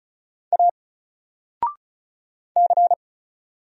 Morse code, heard as a keyed tone, about 36 wpm, 700 Hz. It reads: A C
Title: Single Letter-Number Alt 35wpm